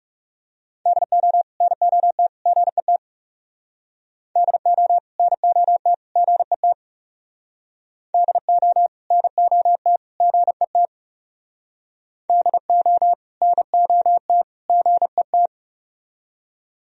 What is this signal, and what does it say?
Morse code, heard as a keyed tone, about 35 wpm, 700 Hz. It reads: DO NOT GET DO NOT GET DO NOT GET DO NOT GET
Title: Speed-Racing - Sets of 3 Words 30wpm